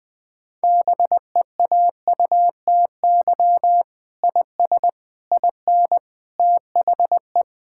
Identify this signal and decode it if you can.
Morse code, heard as a keyed tone, about 20 wpm, 700 Hz. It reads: BEAUTY IS IN THE
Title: Sentences with 7-Word Phrases - 20wpm